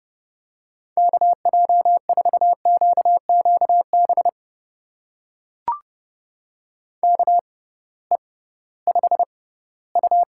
Morse code, heard as a keyed tone, about 30 wpm, spaced as for 17 wpm, 700 Hz. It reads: KJ4QQB KE5U
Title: USA Call Signs - 6 Characters - 6x Character Spacing - 30wpm